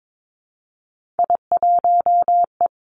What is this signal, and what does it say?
Morse code, heard as a keyed tone, about 22 wpm, 700 Hz. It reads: I1E
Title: Three Challenging Letter-Number Combinations 22wpm